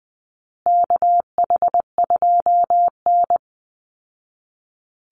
Morse code, heard as a keyed tone, about 20 wpm, 700 Hz. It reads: KH2N
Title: Call Signs Repeated 3x - 20wpm